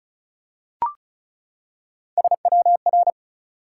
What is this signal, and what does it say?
Morse code, heard as a keyed tone, about 36 wpm, 700 Hz. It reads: SWR
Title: Top 180 QSO Elements 35wpm